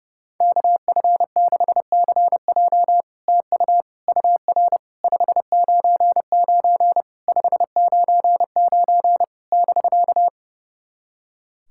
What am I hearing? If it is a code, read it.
Morse code, heard as a keyed tone, about 30 wpm, 700 Hz. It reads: KF6CJ TU UR 599 599 <BK>
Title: SOTA QSO Practice 30wpm